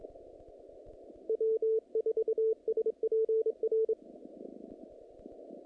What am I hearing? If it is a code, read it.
Morse code, heard as a keyed tone, about 28 wpm, 440 Hz. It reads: W4SPR
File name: contests / naqp-cw-k5tr-op-k5pi- -2100